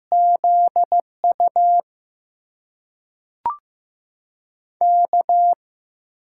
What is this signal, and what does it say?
Morse code, heard as a keyed tone, about 15 wpm, spaced as for 5 wpm, 700 Hz. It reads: ZUK
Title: USA Call Signs - 5 Characters - 6x Character Spacing - 15wpm